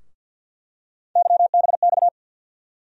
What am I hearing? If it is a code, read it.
Morse code, heard as a keyed tone, about 51 wpm, 700 Hz. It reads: YBX
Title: USA Call Sign Postfixes - 3 Letters 50wpm